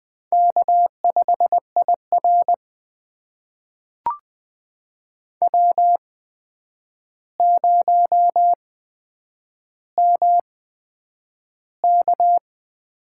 Morse code, heard as a keyed tone, about 20 wpm, spaced as for 9 wpm, 700 Hz. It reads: K5IR W0MK
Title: USA Call Signs - 4 Characters - 8x Character Spacing - 20wpm